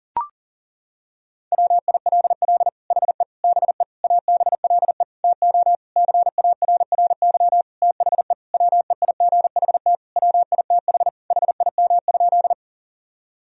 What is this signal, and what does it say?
Morse code, heard as a keyed tone, about 40 wpm, 700 Hz. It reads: WILL HE BE ABLE TO CARRY THE WEIGHT WITH HIM?
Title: Sentences from Top 1000 Words - Review Part 3 - 40wpm